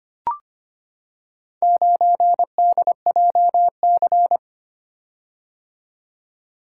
Morse code, heard as a keyed tone, about 25 wpm, 700 Hz. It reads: 9DJC
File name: practice / 25wpm / Four Letter-Number Combinations - 25wpm